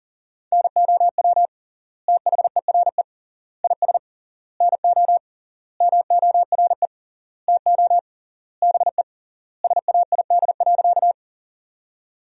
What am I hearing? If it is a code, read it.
Morse code, heard as a keyed tone, about 40 wpm, 700 Hz. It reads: NOW THERE IS NO MORE TO BE SAID.
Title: 3x Word-Spacing - Sentences from Top 100 Words - 40wpm